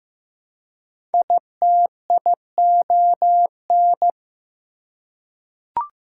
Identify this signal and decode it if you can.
Morse code, heard as a keyed tone, about 15 wpm, 700 Hz. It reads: ITION
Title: Prefixes and Suffixes 15wpm